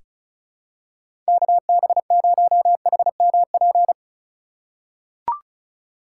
Morse code, heard as a keyed tone, about 34 wpm, 700 Hz. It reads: KB0HMP
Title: USA Call Signs - 3x Character Spacing - 35wpm